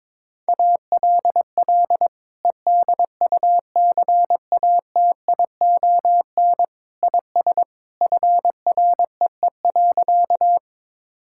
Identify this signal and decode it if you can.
Morse code, heard as a keyed tone, about 22 wpm, 700 Hz. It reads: ALL EDUCATION IS FREE.
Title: Sentences from Top 1000 Words - Review Part 6 - 22wpm